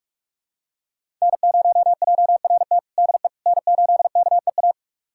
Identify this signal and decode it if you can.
Morse code, heard as a keyed tone, about 46 wpm, 700 Hz. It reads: N0JRT DE N8KEA
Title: USA Callsign DE Callsign - 5 Characters - 45wpm